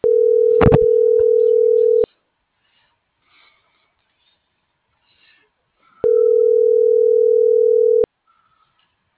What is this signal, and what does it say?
Ringback tone (North America), 440 + 480 Hz.